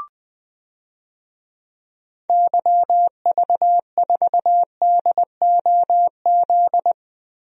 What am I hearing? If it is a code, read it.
Morse code, heard as a keyed tone, about 21 wpm, 700 Hz. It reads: YV4DOZ
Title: DXCC-100 Call Signs - 6 Characters - 20wpm